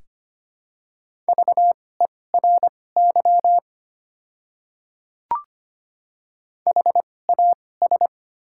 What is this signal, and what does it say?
Morse code, heard as a keyed tone, about 25 wpm, spaced as for 18 wpm, 700 Hz. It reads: VERY HAS
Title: 2x Character Spacing - Top 200 Words - 25wpm